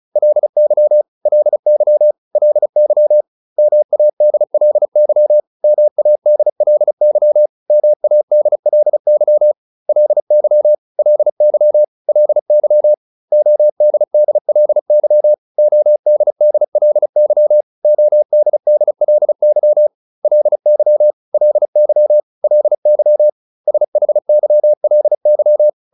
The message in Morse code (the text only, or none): LY LY LY MADLY MADLY MADLY LY LY LY ODDLY ODDLY ODDLY LY LY LY SHYLY